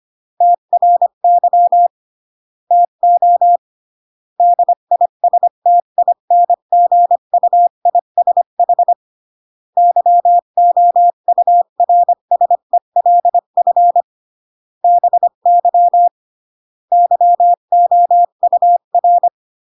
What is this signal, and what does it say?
Morse code, heard as a keyed tone, about 25 wpm, 700 Hz. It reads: TRY TO DISTINGUISH YOURSELF BY YOUR
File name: CW / jodorowsky / 82 commandments 25-20